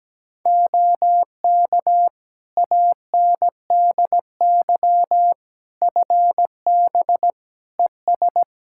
Morse code, heard as a keyed tone, about 17 wpm, 700 Hz. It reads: OK ANDY FB ES
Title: K1USN SST QSO Practice 17wpm